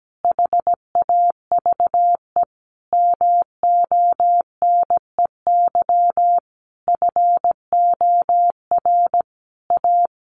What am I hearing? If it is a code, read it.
Morse code, heard as a keyed tone, about 17 wpm, 700 Hz. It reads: HAVE MONEY FOR A
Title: Sentences from Top 1400 Words 17wpm